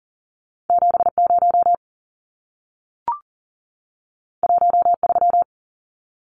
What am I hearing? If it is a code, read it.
Morse code, heard as a keyed tone, about 40 wpm, 700 Hz. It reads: 70 13
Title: Numbers - 2-Digit 40wpm